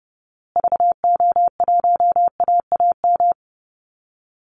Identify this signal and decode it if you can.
Morse code, heard as a keyed tone, about 30 wpm, 700 Hz. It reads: VO1AAM
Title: Call Signs Repeated 3x - 30wpm